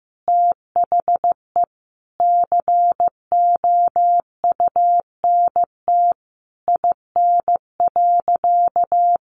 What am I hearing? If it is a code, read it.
Morse code, heard as a keyed tone, about 15 wpm, 700 Hz. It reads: THE COUNT IN.
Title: Sentences from Top 1500 Words 15wpm